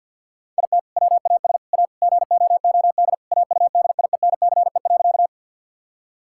Morse code, heard as a keyed tone, about 50 wpm, 700 Hz. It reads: IT WAS A GOOD AUDIENCE.
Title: Sentences from Top 1000 Words - Review Part 5 - 50wpm